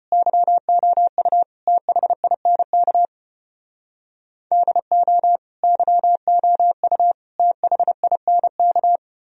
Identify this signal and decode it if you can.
Morse code, heard as a keyed tone, about 32 wpm, 700 Hz. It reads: YOU THINK DO YOU THINK